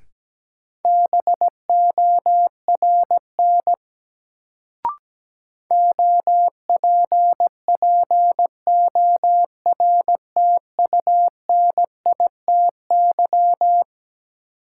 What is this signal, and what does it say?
Morse code, heard as a keyed tone, about 17 wpm, 700 Hz. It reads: BORN OPPORTUNITY
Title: Sentences from Top 1000 Words - New Word Drill - 17wpm